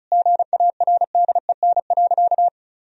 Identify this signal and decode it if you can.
Morse code, heard as a keyed tone, about 36 wpm, 700 Hz. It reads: GARDEN.
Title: Sentences from Top 1000 Words - Review Part 4 - 35wpm